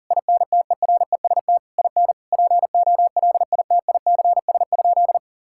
Morse code, heard as a keyed tone, about 40 wpm, 700 Hz. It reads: INTEREST IN POLITICS?
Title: Sentences from Top 1000 Words - Review Part 6 - 40wpm